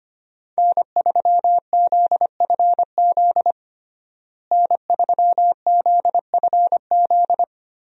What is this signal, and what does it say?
Morse code, heard as a keyed tone, about 25 wpm, 700 Hz. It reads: N3ZFZ N3ZFZ
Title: USA Call Signs - 5 Characters - Repeated 2x - 25wpm